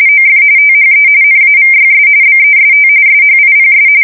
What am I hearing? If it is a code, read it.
RTTY, about 45.45 baud. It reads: OF THE NAVYS BEST SHIPS
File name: fsk170 / usnavy2